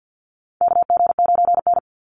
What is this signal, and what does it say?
Morse code, heard as a keyed tone, about 48 wpm, 700 Hz. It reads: KG9N
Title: Call Signs Repeated 3x - 50wpm